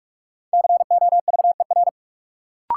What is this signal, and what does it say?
Morse code, heard as a keyed tone, about 46 wpm, 700 Hz. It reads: COVER
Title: Sentences from Top 500 Words - New Word Drill - 45wpm